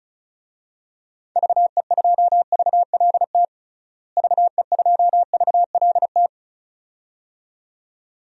Morse code, heard as a keyed tone, about 34 wpm, 700 Hz. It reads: VE2VLT VE2VLT
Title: DXCC-100 Call Signs - 6 Characters - Repeated 2x - 35wpm